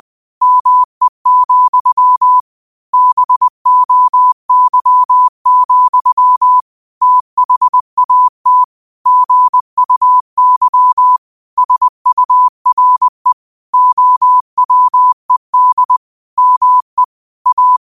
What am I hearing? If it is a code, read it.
Morse code, heard as a keyed tone, about 20 wpm, 1000 Hz. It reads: ME, BOY, THAT GUY SURE OWED ME A
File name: qotdfiles / 20wpm